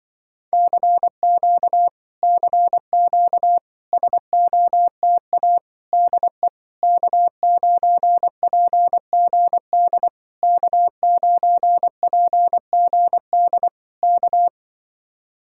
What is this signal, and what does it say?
Morse code, heard as a keyed tone, about 23 wpm, 700 Hz. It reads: CQ CQ SOTA DE K9PGD K9PGD K
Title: Speed-Racing Common QSO Phrases - 20wpm